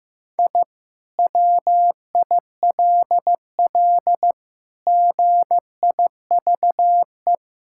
Morse code, heard as a keyed tone, about 16 wpm, 700 Hz. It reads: I WILL GIVE
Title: Sentences from Top 1000 Words - Review Part 6 - 15wpm